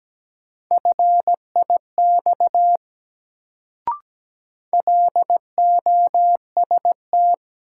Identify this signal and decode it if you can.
Morse code, heard as a keyed tone, about 17 wpm, 700 Hz. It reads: FIX LOST
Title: Sentences from Top 1400 Words - New Word Drill - 17wpm